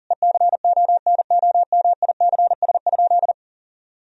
Morse code, heard as a keyed tone, about 37 wpm, 700 Hz. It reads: ECONOMICS?
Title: Sentences from Top 1000 Words - Review Part 3 - 40wpm